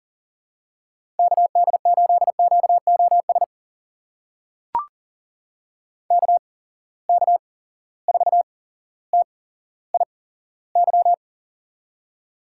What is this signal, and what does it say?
Morse code, heard as a keyed tone, about 40 wpm, spaced as for 17 wpm, 700 Hz. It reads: KD8QOS KK4TIY
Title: USA Call Signs - 8x Character Spacing - 40wpm